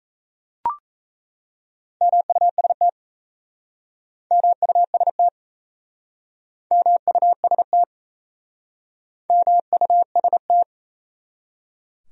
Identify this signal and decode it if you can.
Morse code, heard as a keyed tone, about 37 wpm, 690 Hz. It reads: MUST MUST MUST MUST